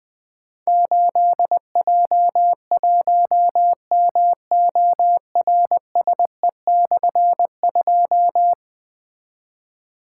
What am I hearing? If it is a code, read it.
Morse code, heard as a keyed tone, about 20 wpm, 700 Hz. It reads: 8J1MORSE/2
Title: Difficult Call Signs 20wpm